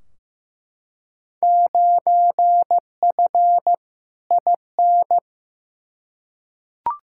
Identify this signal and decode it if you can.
Morse code, heard as a keyed tone, about 15 wpm, 700 Hz. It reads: 9F IN
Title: Field Day Exchanges Encore - 15wpm